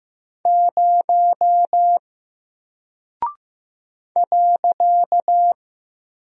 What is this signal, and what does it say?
Morse code, heard as a keyed tone, about 15 wpm, 700 Hz. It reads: 0 .
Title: Single Character (Letter:Number:Punctuation:Prosign) Alt 15wpm